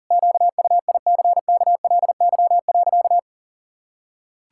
Morse code, heard as a keyed tone, about 40 wpm, 700 Hz. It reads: QUICKLY.
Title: Pangram Challenge v2 - 40wpm